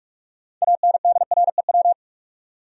Morse code, heard as a keyed tone, about 47 wpm, 700 Hz. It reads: ANDREW
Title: USA Common Names - 45wpm